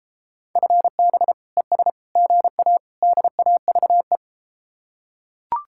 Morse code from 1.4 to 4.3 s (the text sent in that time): ES GA DAVE